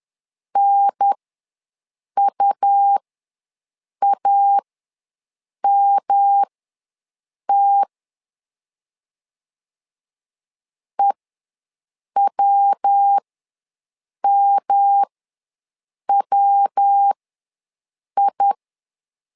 Morse code, heard as a keyed tone, about 10 wpm, spaced as for 6 wpm, 800 Hz. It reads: NUAMT EWMWI
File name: LEZIONI / 3 MWRO / 3 MW ws 10wpm 2213